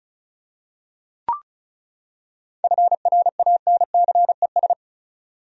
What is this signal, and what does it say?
Morse code, heard as a keyed tone, about 34 wpm, 700 Hz. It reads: FRANCES
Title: USA Common Names - 35wpm